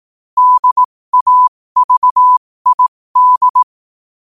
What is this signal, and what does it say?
Morse code, heard as a keyed tone, about 18 wpm, spaced as for 15 wpm, 1000 Hz. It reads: DAVID
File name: qotdfiles / 15wpm